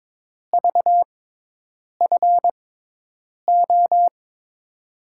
Morse code, heard as a keyed tone, about 21 wpm, spaced as for 8 wpm, 700 Hz. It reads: VFO
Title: USA Call Signs - 6x Character Spacing - 22wpm